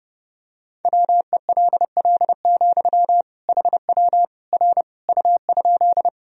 Morse code, heard as a keyed tone, about 32 wpm, 700 Hz. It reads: WELL, HW R U?
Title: Sentences from Top 200 Words and Abbreviations 30wpm